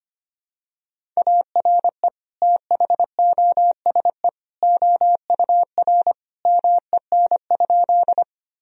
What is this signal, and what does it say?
Morse code, heard as a keyed tone, about 25 wpm, 700 Hz. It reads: ARE THOSE OUR MEN?
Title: Sentences from Top 1000 Words - Review Part 1 - 25wpm